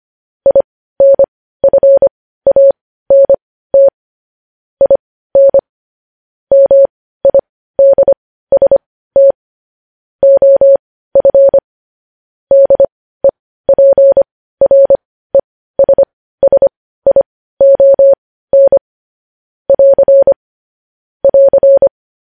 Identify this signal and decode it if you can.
Morse code, heard as a keyed tone, about 25 wpm, spaced as for 15 wpm, 550 Hz. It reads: INFANT IN MIDST OF DEPRESSION <AR> <AR>
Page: News Headlines in Morse Code at 15 WPM